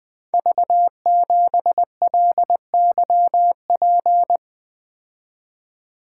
Morse code, heard as a keyed tone, about 20 wpm, 700 Hz. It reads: V7LYP
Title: Code Groups - 5 Letter:Numbers - 20wpm